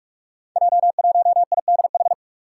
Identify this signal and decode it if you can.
Morse code, heard as a keyed tone, about 43 wpm, 700 Hz. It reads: J1IDH
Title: Five Challenging Letter-Number Combinations 45wpm